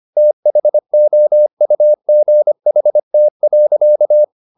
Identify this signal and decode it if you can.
Morse code, heard as a keyed tone, about 25 wpm, 600 Hz. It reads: THOUGHT.
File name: the old man and the sea part 1 25wpm